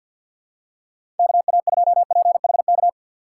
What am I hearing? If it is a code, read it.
Morse code, heard as a keyed tone, about 53 wpm, 700 Hz. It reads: KA2PHK
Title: USA Call Signs - 6 Characters - 3x Character Spacing - 50wpm